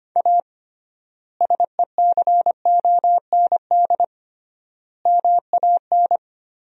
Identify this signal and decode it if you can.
Morse code, heard as a keyed tone, about 25 wpm, 700 Hz. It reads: A SECOND MAN
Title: 3x Word-Spacing - Sentences from Top 300 Words - 25wpm